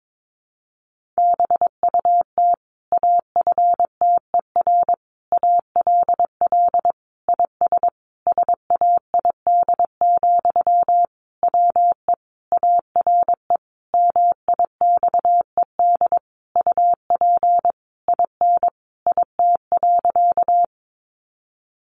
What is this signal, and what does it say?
Morse code, heard as a keyed tone, about 22 wpm, 700 Hz. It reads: BUT AFTER ALL IS SAID, WE ARE MIXED UP IN IT.
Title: Sentences from Top 1500 Words 22wpm